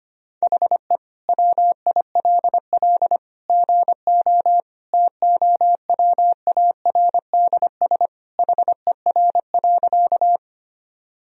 Morse code, heard as a keyed tone, about 25 wpm, 700 Hz. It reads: HE WILL GO TOWARDS HER.
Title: Sentences from Top 1000 Words - Review Part 2 - 25wpm